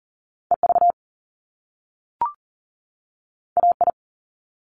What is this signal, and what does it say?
Morse code, heard as a keyed tone, about 42 wpm, 700 Hz. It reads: EV AI
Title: Top Two Letter Combinations in Words 40wpm